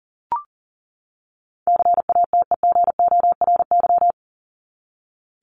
Morse code, heard as a keyed tone, about 40 wpm, 700 Hz. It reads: CATEGORY
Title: Top 751-1000 Words 40wpm